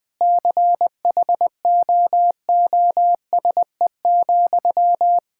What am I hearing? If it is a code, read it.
Morse code, heard as a keyed tone, about 21 wpm, 700 Hz. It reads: CHOOSE,
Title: Sentences from Top 1000 Words - Review Part 6 - 20wpm